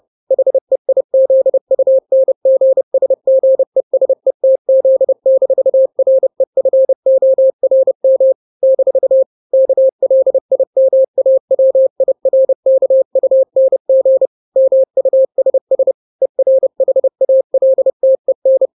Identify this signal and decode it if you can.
Morse code, heard as a keyed tone, about 30 wpm, 530 Hz. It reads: HEIZUNGSGESETZ-REFORM - KLIMAWIRKUNG MUSS ERHALTEN
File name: daily / 29wpm